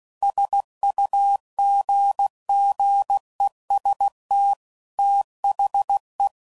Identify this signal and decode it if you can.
Morse code, heard as a keyed tone, about 16 wpm, 800 Hz. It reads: SUGGEST THE
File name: CW Code Groups @15WPM / Technical Subjects / OR Gate